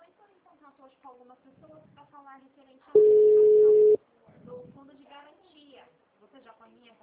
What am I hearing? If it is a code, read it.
Ringback tone (Europe, 425 Hz), with speech.
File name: out- -2009- - - .65933